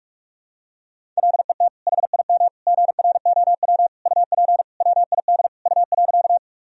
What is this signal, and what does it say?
Morse code, heard as a keyed tone, about 41 wpm, 700 Hz. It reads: LET HIM GROW UP WID U.
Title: Sentences from Top 300 Words and Abbreviations 45wpm